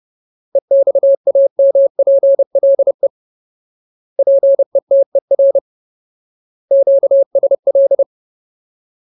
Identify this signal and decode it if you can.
Morse code, heard as a keyed tone, about 30 wpm, 550 Hz. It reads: EXAMPLE PETER QSL
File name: Ultimate Word List / 30WPM Ultimate